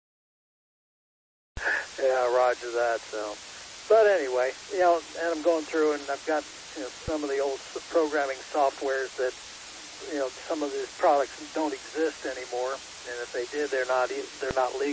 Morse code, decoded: none

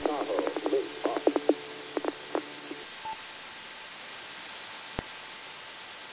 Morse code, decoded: none

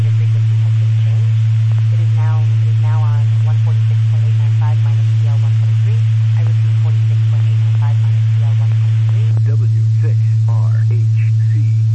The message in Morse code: none

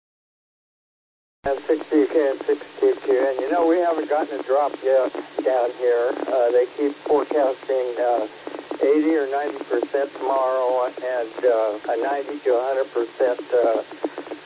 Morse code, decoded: none

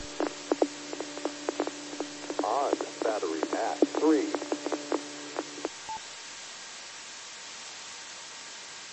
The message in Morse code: none

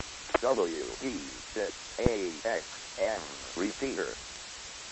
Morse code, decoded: none

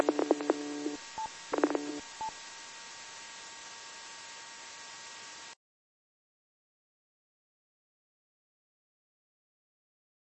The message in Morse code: none

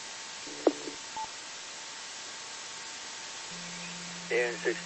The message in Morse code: none

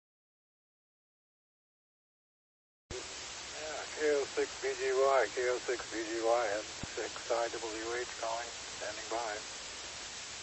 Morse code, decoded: none

